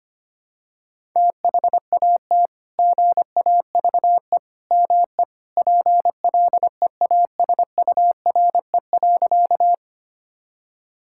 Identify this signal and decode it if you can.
Morse code, heard as a keyed tone, about 25 wpm, 700 Hz. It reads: THAT GAVE ME PLEASURE.
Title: Sentences from Top 1600 Words 25wpm